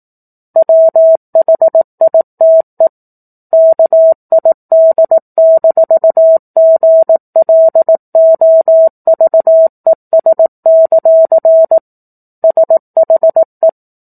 Morse code, heard as a keyed tone, about 18 wpm, 650 Hz. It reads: WHITE KID-GLOVES; SHE
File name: 18 WPM